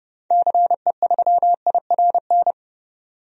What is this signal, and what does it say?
Morse code, heard as a keyed tone, about 30 wpm, 700 Hz. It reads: CE3IRN
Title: DXCC-100 Call Signs - 6 Characters - 30wpm